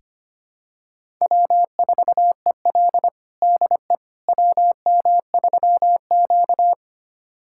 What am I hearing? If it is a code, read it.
Morse code, heard as a keyed tone, about 25 wpm, 700 Hz. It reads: W4EL DE WM3Q
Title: USA Callsign DE Callsign - 4 Characters - 25wpm